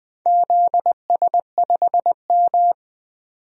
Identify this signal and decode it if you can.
Morse code, heard as a keyed tone, about 21 wpm, 700 Hz. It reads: ZS5M
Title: DXCC-100 Call Signs - 4 Characters - 20wpm